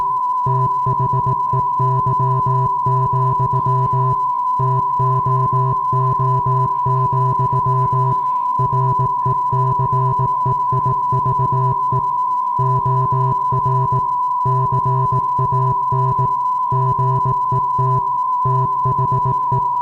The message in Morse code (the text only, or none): THEY, TOO, RECEIVE OR CAN GET THE